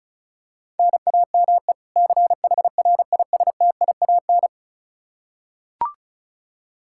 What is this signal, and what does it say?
Morse code, heard as a keyed tone, about 36 wpm, 700 Hz. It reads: NAME CHRISTIAN
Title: K1USN SST QSO Practice 35wpm